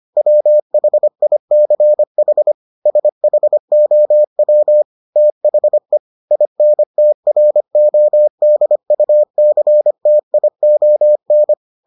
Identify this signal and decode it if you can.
Morse code, heard as a keyed tone, about 25 wpm, 600 Hz. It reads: WHICH SHOW THE INTRODUCTION